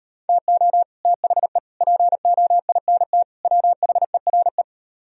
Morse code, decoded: TO THE POINT WHERE

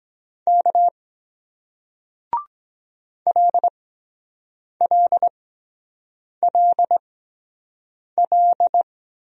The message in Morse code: K LLLL

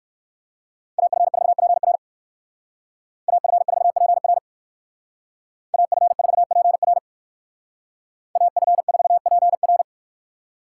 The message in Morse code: AF4PR AF4PR AF4PR AF4PR

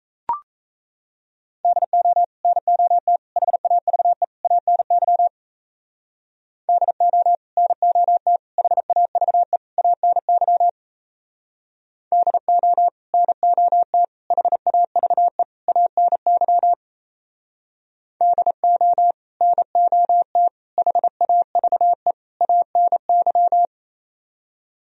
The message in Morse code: DO NOT HAVE ANY DO NOT HAVE ANY DO NOT HAVE ANY DO NOT HAVE ANY

